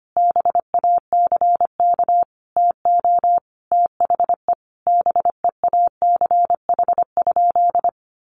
Morse code, heard as a keyed tone, about 25 wpm, 700 Hz. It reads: BACK TO THE BEACH?
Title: Sentences from Top 1500 Words 25wpm